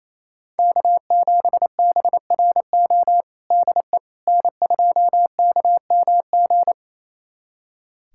Morse code, heard as a keyed tone, about 28 wpm, 700 Hz. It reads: K7BRO DE N2KMG
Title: USA Callsign DE Callsign - 5 Characters - 28wpm